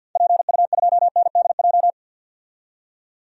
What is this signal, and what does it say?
Morse code, heard as a keyed tone, about 53 wpm, 700 Hz. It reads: PU2NDJ
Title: DXCC-100 Call Signs - 6 Characters - 50wpm